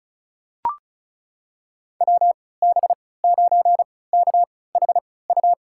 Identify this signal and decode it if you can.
Morse code, heard as a keyed tone, about 36 wpm, spaced as for 20 wpm, 700 Hz. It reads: WB9KHU
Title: USA Call Signs - 3x Character Spacing - 35wpm